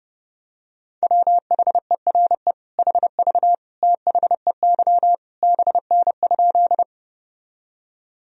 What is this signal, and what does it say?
Morse code, heard as a keyed tone, about 30 wpm, 700 Hz. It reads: WHERE HV THEY BN?
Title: Sentences from Top 200 Words and Abbreviations 30wpm